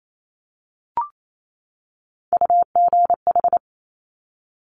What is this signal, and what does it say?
Morse code, heard as a keyed tone, about 28 wpm, 700 Hz. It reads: UGH
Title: Top Three Letter Combinations in Words 28wpm